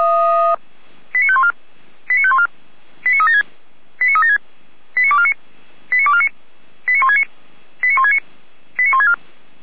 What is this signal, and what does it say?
Five-tone selective call (ZVEI), heard as quick runs of five tones, tones of about 70 ms. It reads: 89514 89514 89376 89376 89249 89249 89169 89169 89164
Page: SDR/ZVEI